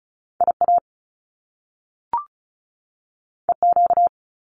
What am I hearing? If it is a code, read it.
Morse code, heard as a keyed tone, about 36 wpm, 700 Hz. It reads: I A E Q
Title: Top Two Letter Combinations in Words 35wpm